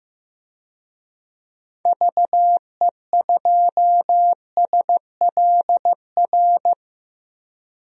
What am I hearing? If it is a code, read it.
Morse code, heard as a keyed tone, about 15 wpm, 700 Hz. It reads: VE2SLR
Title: DXCC-100 Call Signs - 6 Characters - 15wpm